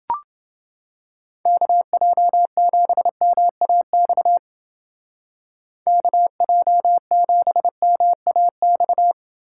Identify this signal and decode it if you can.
Morse code, heard as a keyed tone, about 28 wpm, 700 Hz. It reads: KJ7MAX KJ7MAX